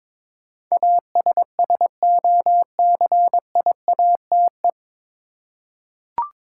Morse code, heard as a keyed tone, about 22 wpm, 700 Hz. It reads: ASSOCIATE
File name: 9 Letter Words - First 3 Letters Given - 22wpm